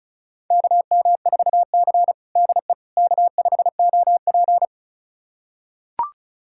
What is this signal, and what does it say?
Morse code, heard as a keyed tone, about 36 wpm, 700 Hz. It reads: KM4C DE K5OP
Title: USA Callsign DE Callsign - 4 Characters - 35wpm